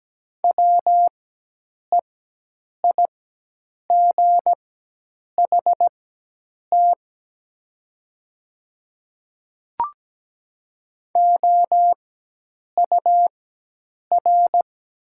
Morse code, heard as a keyed tone, about 17 wpm, spaced as for 8 wpm, 700 Hz. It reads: WEIGHT OUR
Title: 4x Character Spacing - Top 500 Words - 17wpm